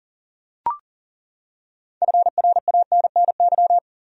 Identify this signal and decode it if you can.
Morse code, heard as a keyed tone, about 40 wpm, 700 Hz. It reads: FRANNY
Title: USA Common Names - 40wpm